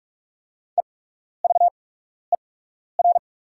Morse code, heard as a keyed tone, about 44 wpm, spaced as for 12 wpm, 690 Hz. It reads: EVER
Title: 8x Character Spacing - Top 500 Words - 45wpm